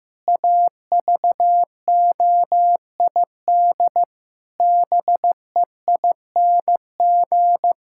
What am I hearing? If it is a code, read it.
Morse code, heard as a keyed tone, about 15 wpm, 700 Hz. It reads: AVOID BEING
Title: Sentences from Top 1000 Words - Review Part 6 - 15wpm